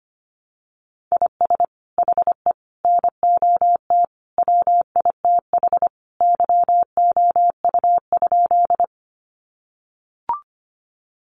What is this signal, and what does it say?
Morse code, heard as a keyed tone, about 25 wpm, 700 Hz. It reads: IS HE NOT WITH YOU?
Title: Sentences from Top 600 Words 25wpm